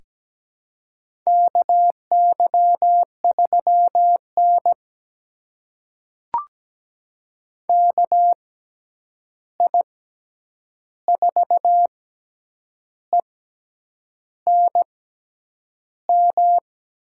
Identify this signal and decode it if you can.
Morse code, heard as a keyed tone, about 17 wpm, spaced as for 8 wpm, 700 Hz. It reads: KY3N KI4ENM
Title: USA Call Signs - 6x Character Spacing - 17wpm